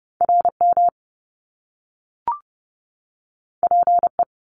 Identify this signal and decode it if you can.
Morse code, heard as a keyed tone, about 30 wpm, 700 Hz. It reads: RM PE
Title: Top Two Letter Combinations in Words 30wpm